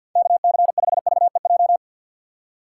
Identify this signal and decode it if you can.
Morse code, heard as a keyed tone, about 48 wpm, 700 Hz. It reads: KC5VEJ
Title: USA Call Signs - 6 Characters - 3x Character Spacing - 50wpm